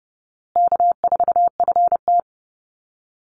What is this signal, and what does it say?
Morse code, heard as a keyed tone, about 30 wpm, 700 Hz. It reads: K4FT
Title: Easy Call Signs Repeated 3x - 30wpm